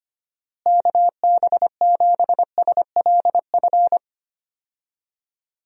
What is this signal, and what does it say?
Morse code, heard as a keyed tone, about 25 wpm, 700 Hz. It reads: KB7SLF
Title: USA Call Signs - 6 Characters - Repeated 2x - 25wpm